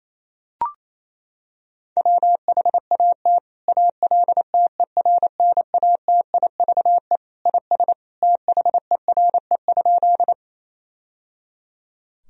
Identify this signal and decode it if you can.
Morse code, heard as a keyed tone, about 28 wpm, 700 Hz. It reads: WHAT ALTERNATIVE IS THERE?
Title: Sentences from Top 1000 Words - Review Part 4 - 28wpm